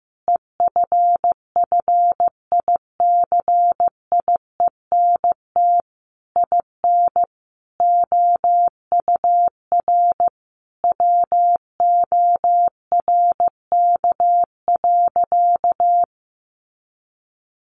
Morse code, heard as a keyed tone, about 15 wpm, 700 Hz. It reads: EFFICIENT IN OUR WORK.
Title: Sentences from Top 1300 Words 15wpm